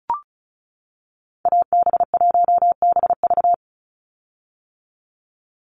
Morse code, heard as a keyed tone, about 34 wpm, 700 Hz. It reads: AB1BV
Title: USA Call Signs 35wpm